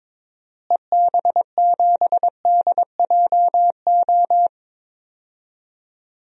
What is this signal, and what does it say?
Morse code, heard as a keyed tone, about 22 wpm, 700 Hz. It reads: EB7DJO
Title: DXCC-100 Call Signs - 6 Characters - Repeated 2x - 22wpm